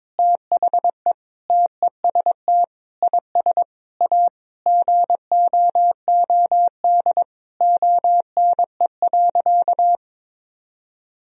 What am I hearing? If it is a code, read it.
Morse code, heard as a keyed tone, about 22 wpm, 700 Hz. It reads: THE TEST IS A GOOD ONE.
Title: Sentences from Top 1000 Words - Review Part 3 - 22wpm